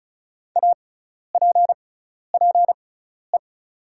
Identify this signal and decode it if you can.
Morse code, heard as a keyed tone, about 34 wpm, spaced as for 12 wpm, 690 Hz. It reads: APPE
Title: 6x Character Spacing - Top 500 Words - 35wpm